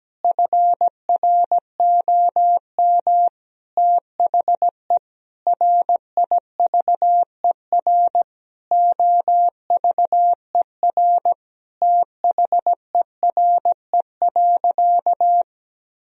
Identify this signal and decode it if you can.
Morse code, heard as a keyed tone, about 17 wpm, 700 Hz. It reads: FROM THE RIVER OVER THERE.
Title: Sentences from Top 1000 Words - Review Part 2 - 17wpm